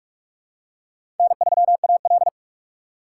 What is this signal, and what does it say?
Morse code, heard as a keyed tone, about 46 wpm, 700 Hz. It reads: N3AL
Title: USA Call Signs - 4x Character Spacing - 45wpm